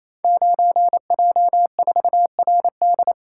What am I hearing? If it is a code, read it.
Morse code, heard as a keyed tone, about 27 wpm, 700 Hz. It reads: 9J4RD
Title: Code Groups - 5 Letter:Numbers - 28wpm